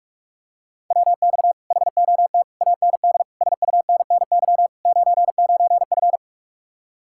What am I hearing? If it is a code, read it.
Morse code, heard as a keyed tone, about 49 wpm, 700 Hz. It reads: WX HOT AND SUNNY 99F